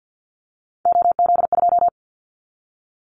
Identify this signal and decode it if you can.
Morse code, heard as a keyed tone, about 49 wpm, 700 Hz. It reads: OZ2
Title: Callsign Prefixes for Top 100 DXCC - 50wpm